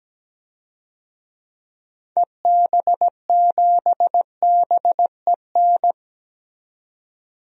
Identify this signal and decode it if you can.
Morse code, heard as a keyed tone, about 17 wpm, 700 Hz. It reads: EB7BEN
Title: DXCC-100 Call Signs - 6 Characters - 17wpm